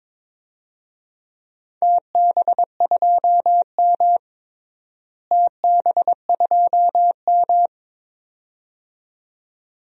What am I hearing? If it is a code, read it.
Morse code, heard as a keyed tone, about 22 wpm, 700 Hz. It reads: TB2M TB2M
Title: DXCC-100 Call Signs - 4 Characters - Repeated 2x - 22wpm